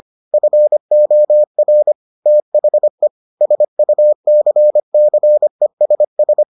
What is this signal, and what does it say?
Morse code, heard as a keyed tone, about 25 wpm, 600 Hz. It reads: FOR THE SUCCESS